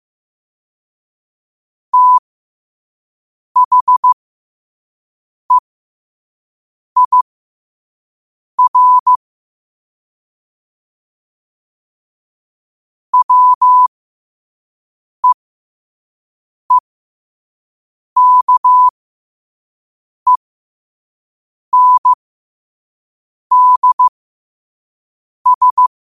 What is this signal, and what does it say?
Morse code, heard as a keyed tone, about 15 wpm, spaced as for 5 wpm, 1000 Hz. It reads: THEIR WEEKENDS